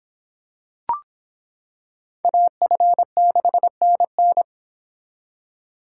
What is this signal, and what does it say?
Morse code, heard as a keyed tone, about 26 wpm, 700 Hz. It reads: AF6NN